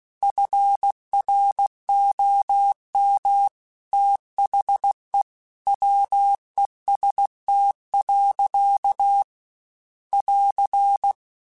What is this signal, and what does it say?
Morse code, heard as a keyed tone, about 16 wpm, 800 Hz. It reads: FROM THE WEST. <AR>
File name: Weather Sets / Weather Sets-mp3s / Weather 5 1 @15